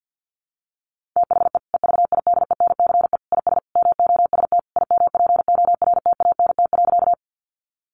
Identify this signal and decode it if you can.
Morse code, heard as a keyed tone, about 50 wpm, 700 Hz. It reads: THE EVIDENCE IS MOST IMPORTANT.
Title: Sentences from Top 1300 Words 50wpm